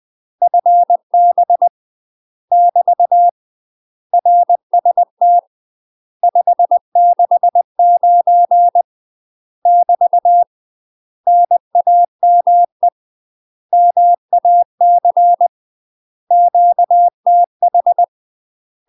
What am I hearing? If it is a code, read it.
Morse code, heard as a keyed tone, about 20 wpm, 700 Hz. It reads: FB <BT> RST 569 <BT> NAME MAC QTH
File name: UK MASTER QSOs TRAINING / 20WPM / UK MASTER QSOs 0012